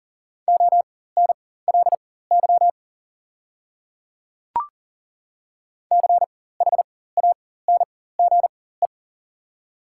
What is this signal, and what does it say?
Morse code, heard as a keyed tone, about 40 wpm, spaced as for 19 wpm, 690 Hz. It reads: ONLY CHANGE
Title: 4x Character Spacing - Top 300 Words - 40wpm